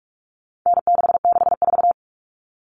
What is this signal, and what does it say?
Morse code, heard as a keyed tone, about 44 wpm, 700 Hz. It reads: N664
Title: All Cut Numbers - 4-Digit 45wpm